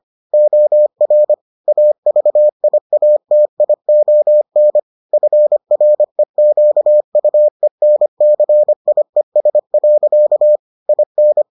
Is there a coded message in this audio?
Yes: Morse code from 0.3 to 11.4 s, heard as a keyed tone, about 25 wpm, 600 Hz. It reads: OR AVIATION FREQUENCIES. IN